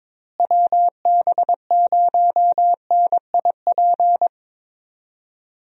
Morse code, heard as a keyed tone, about 22 wpm, 700 Hz. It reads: WB0NIP
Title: USA Call Signs - 6 Characters - Repeated 2x - 22wpm